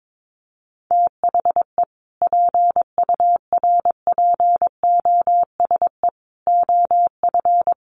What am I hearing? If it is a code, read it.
Morse code, heard as a keyed tone, about 22 wpm, 700 Hz. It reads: THE PURPOSE OF
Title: Sentences from Top 800 Words 22wpm